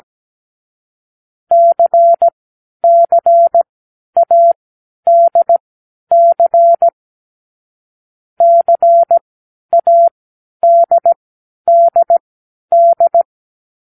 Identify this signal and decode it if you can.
Morse code, heard as a keyed tone, about 17 wpm, spaced as for 10 wpm, 680 Hz. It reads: CCADC CADDD
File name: k7qo / Lesson6(A-D)